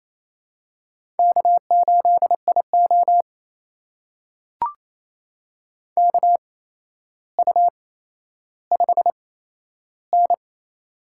Morse code, heard as a keyed tone, about 28 wpm, spaced as for 12 wpm, 700 Hz. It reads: K8IO KU5N
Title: USA Call Signs - 4 Characters - 8x Character Spacing - 28wpm